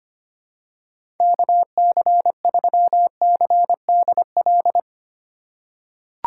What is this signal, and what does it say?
Morse code, heard as a keyed tone, about 25 wpm, 700 Hz. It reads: KC3CDL
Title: USA Call Signs - 6 Characters - 3x Character Spacing - 25wpm